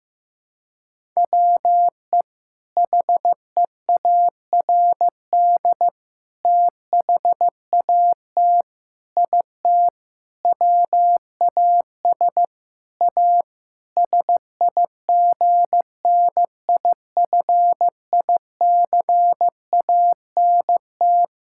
Morse code, heard as a keyed tone, about 15 wpm, 700 Hz. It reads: WE HEARD THAT IT WAS A SIGNIFICANT
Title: Sentences from Top 1000 Words - Review Part 6 - 15wpm